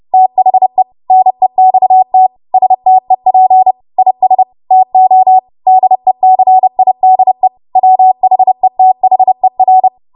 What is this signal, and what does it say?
Morse code, heard as a keyed tone, about 30 wpm, 750 Hz. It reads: THE NEXT STEP IS TO DECIDE WHETHER
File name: Morse / 30WPM